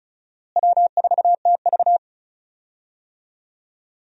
Morse code, heard as a keyed tone, about 36 wpm, 700 Hz. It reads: W4TV
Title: USA Call Signs - 4 Characters - Repeated 2x - 35wpm